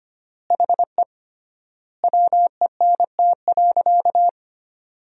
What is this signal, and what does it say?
Morse code, heard as a keyed tone, about 25 wpm, 700 Hz. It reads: HE WENT.
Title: 3x Word-Spacing - Sentences from Top 200 Words - 25wpm